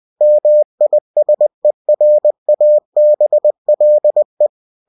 Morse code, heard as a keyed tone, about 20 wpm, 600 Hz. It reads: MISERABLE
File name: cw-books / winnie-the-pooh / winnie-20wpm-page-0048